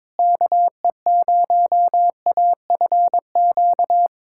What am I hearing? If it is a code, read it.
Morse code, heard as a keyed tone, about 22 wpm, 700 Hz. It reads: KE0AFQ
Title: USA Callsign DE Callsign - 6 Characters - 22wpm